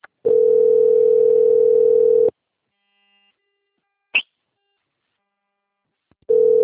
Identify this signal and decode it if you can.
Ringback tone (North America), 440 + 480 Hz.